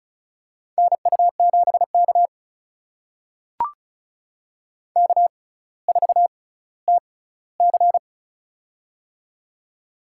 Morse code, heard as a keyed tone, about 34 wpm, spaced as for 18 wpm, 700 Hz. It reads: NU7K K4TC
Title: USA Call Signs - 4 Characters - 6x Character Spacing - 35wpm